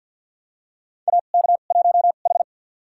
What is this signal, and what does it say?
Morse code, heard as a keyed tone, about 46 wpm, spaced as for 35 wpm, 700 Hz. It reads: AK1H
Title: USA Call Signs - 4 Characters - 2x Character Spacing - 50wpm